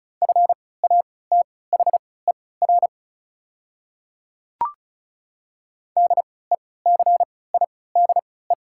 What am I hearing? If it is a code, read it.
Morse code, heard as a keyed tone, about 35 wpm, spaced as for 20 wpm, 690 Hz. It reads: FATHER DECIDE
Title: 3x Character Spacing - Top 500 Words - 35wpm